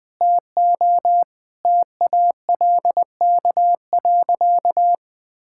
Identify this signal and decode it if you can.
Morse code, heard as a keyed tone, about 21 wpm, 700 Hz. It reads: TO TALK.
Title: Sentences with 4-Word Phrases - Spoken Beginning - 20wpm